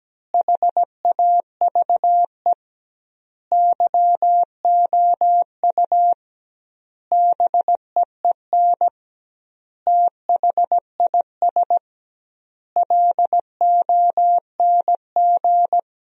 Morse code, heard as a keyed tone, about 17 wpm, 700 Hz. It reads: HAVE YOU BEEN THIS LONG
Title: 2x Word-Spacing - Sentences from Top 300 Words - 17wpm